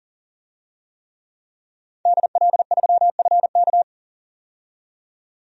Morse code, heard as a keyed tone, about 40 wpm, 700 Hz. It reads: DL3FK
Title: DXCC-100 Call Signs - 5 Characters - 40wpm